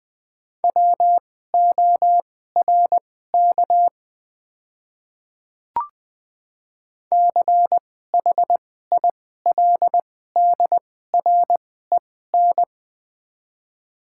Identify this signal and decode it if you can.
Morse code, heard as a keyed tone, about 20 wpm, spaced as for 15 wpm, 690 Hz. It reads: WORK CHILDREN
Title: 2x Character Spacing - Top 500 Words - 20wpm